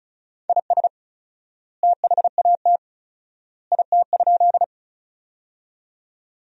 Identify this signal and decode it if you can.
Morse code, heard as a keyed tone, about 34 wpm, 700 Hz. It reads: IS THAT IT?
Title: 4x Word-Spacing - Sentences from Top 100 Words - 35wpm